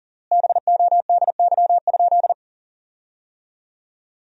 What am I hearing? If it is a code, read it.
Morse code, heard as a keyed tone, about 40 wpm, 700 Hz. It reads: BODY?
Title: 3x Word-Spacing - Sentences from Top 400 Words - 40wpm